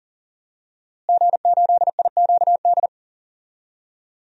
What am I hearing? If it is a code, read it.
Morse code, heard as a keyed tone, about 39 wpm, 700 Hz. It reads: G8IQD
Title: DXCC-100 Call Signs - 5 Characters - Repeated 2x - 40wpm